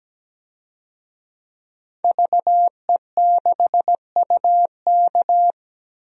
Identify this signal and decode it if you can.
Morse code, heard as a keyed tone, about 17 wpm, 700 Hz. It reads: VE6UK
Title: DXCC-100 Call Signs - 5 Characters - 17wpm